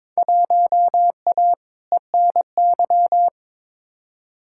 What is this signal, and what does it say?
Morse code, heard as a keyed tone, about 22 wpm, 700 Hz. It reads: 1A ENY
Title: Field Day Exchanges Encore - 22wpm